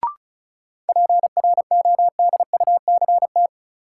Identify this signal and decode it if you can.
Morse code, heard as a keyed tone, about 36 wpm, 700 Hz. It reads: PRODUCT
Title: Sentences from Top 1000 Words - Review Part 2 - New Word Drill - 35wpm